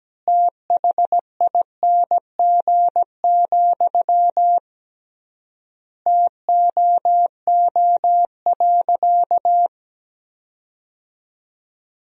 Morse code, heard as a keyed tone, about 17 wpm, 700 Hz. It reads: THING, TOO.
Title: 3x Word-Spacing - Sentences from Top 200 Words - 17wpm